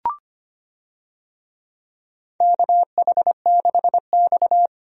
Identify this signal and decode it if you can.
Morse code, heard as a keyed tone, about 25 wpm, 700 Hz. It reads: KH6X
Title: DXCC-100 Call Signs - 4 Characters - 25wpm